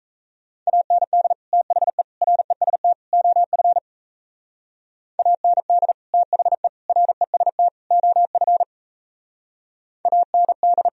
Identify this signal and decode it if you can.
Morse code, heard as a keyed tone, about 36 wpm, 700 Hz. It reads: AND THE REST OF AND THE REST OF AND